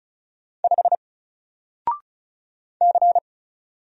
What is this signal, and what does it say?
Morse code, heard as a keyed tone, about 35 wpm, 700 Hz. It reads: 5 C
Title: Single Letter-Number Alt 35wpm